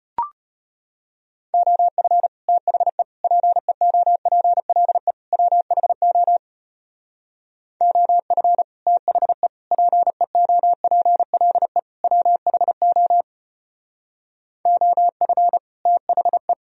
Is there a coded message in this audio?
Yes: Morse code from 1.5 to 16.5 s, heard as a keyed tone, about 32 wpm, 700 Hz. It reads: OF THE PEOPLE WHO OF THE PEOPLE WHO OF THE